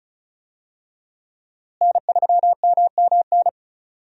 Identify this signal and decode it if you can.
Morse code, heard as a keyed tone, about 34 wpm, 700 Hz. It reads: N3MMN
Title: USA Call Signs - 5 Characters - Repeated 3x - 35wpm